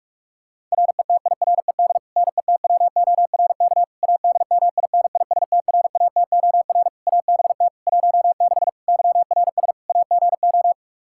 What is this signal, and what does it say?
Morse code, heard as a keyed tone, about 42 wpm, 700 Hz. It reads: RETIRED NETWORK ADMINISTRATOR ABT 16 YRS AGO
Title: Practice QSO - Exchange 3 - More Info - 45wpm